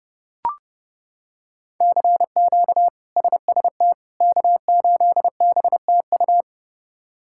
Morse code, heard as a keyed tone, about 30 wpm, 700 Hz. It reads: CQ SST K8BTU
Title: K1USN SST QSO Practice - 30wpm